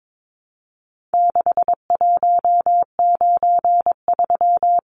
Morse code, heard as a keyed tone, about 22 wpm, 700 Hz. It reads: 6193
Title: Numbers - 4-Digit 22wpm